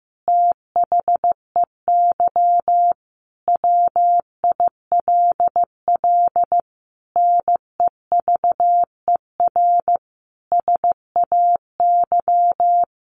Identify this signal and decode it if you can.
Morse code, heard as a keyed tone, about 15 wpm, 700 Hz. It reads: THEY WILL NEVER SAY
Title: Sentences from Top 1400 Words 15wpm